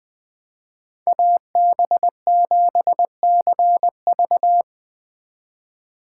Morse code, heard as a keyed tone, about 20 wpm, 700 Hz. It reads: AB7CV
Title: Speed Racing - USA Call Signs 17wpm